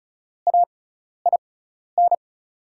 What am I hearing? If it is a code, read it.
Morse code, heard as a keyed tone, about 34 wpm, spaced as for 12 wpm, 690 Hz. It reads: AIN
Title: 6x Character Spacing - Top 300 Words - 35wpm